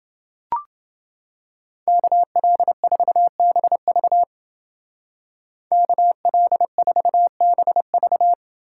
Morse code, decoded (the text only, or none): KL4BV KL4BV